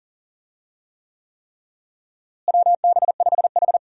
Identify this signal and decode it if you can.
Morse code, heard as a keyed tone, about 40 wpm, 700 Hz. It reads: WB5H
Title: USA Call Signs - 4 Characters - Repeated 3x - 40wpm